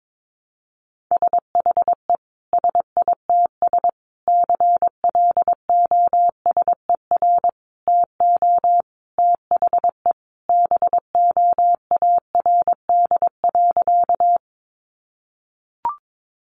Morse code, heard as a keyed tone, about 22 wpm, 700 Hz. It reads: SHE SITS CLOSER TO THE BOARD.
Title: Additional Easy Phrases 22wpm